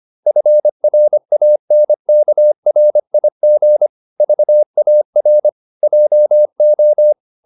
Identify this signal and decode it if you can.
Morse code, heard as a keyed tone, about 25 wpm, 600 Hz. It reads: FRANKRIG VAR JO
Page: Ferie 25 wpm